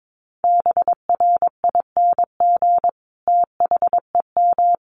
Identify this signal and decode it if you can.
Morse code, heard as a keyed tone, about 22 wpm, 700 Hz. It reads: BRING THEM
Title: Sentences from Top 500 Words 22wpm